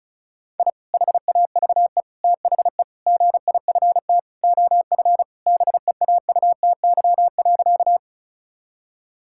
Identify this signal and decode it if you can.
Morse code, heard as a keyed tone, about 36 wpm, 700 Hz. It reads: I HAVE THE GIFT OF BEAUTY.
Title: Sentences from Top 1000 Words - Review Part 6 - 35wpm